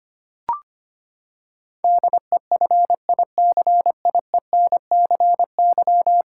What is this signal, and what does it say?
Morse code, heard as a keyed tone, about 25 wpm, 700 Hz. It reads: DEFICIENCY
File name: Ten Letter Words - 25wpm